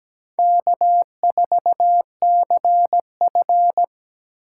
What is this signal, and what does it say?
Morse code, heard as a keyed tone, about 17 wpm, 700 Hz. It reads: K4CF
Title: USA Call Signs - 4 Characters - 2x Character Spacing - 17wpm